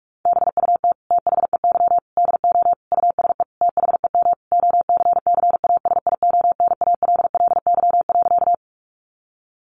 Morse code, heard as a keyed tone, about 44 wpm, 700 Hz. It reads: BUT THEY DO USE THEM OCCASIONALLY.
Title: Sentences from Top 1300 Words 45wpm